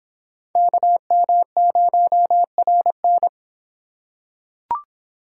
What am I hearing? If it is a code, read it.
Morse code, heard as a keyed tone, about 23 wpm, 700 Hz. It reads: KM0RN